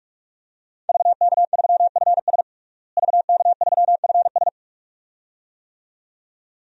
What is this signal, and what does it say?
Morse code, heard as a keyed tone, about 46 wpm, 700 Hz. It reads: VK3FS VK3FS
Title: USA Call Signs - 5 Characters - Repeated 2x - 45wpm